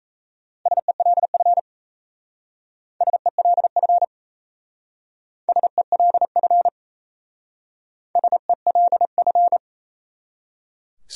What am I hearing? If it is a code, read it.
Morse code, heard as a keyed tone, about 37 wpm, 700 Hz. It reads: SELF SELF SELF SELF